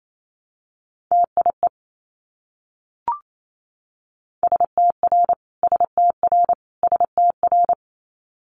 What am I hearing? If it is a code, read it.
Morse code, heard as a keyed tone, about 28 wpm, 700 Hz. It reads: TIE STR STR STR
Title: Multi-letter Phonemes Repeated 3x 28wpm